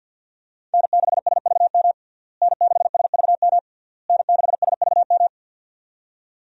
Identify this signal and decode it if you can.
Morse code, heard as a keyed tone, about 48 wpm, 700 Hz. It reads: N6SVM N6SVM N6SVM
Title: USA Call Signs - 5 Characters - Repeated 3x - 50wpm